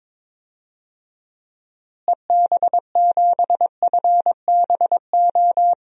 Morse code, heard as a keyed tone, about 22 wpm, 700 Hz. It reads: EB7FBO
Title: DXCC-100 Call Signs - 6 Characters - Repeated 2x - 22wpm